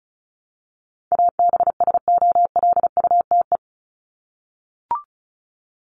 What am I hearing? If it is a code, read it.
Morse code, heard as a keyed tone, about 34 wpm, 700 Hz. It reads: ABSOLUTE
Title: LY Suffix - Top 100 Words with Base Words 35wpm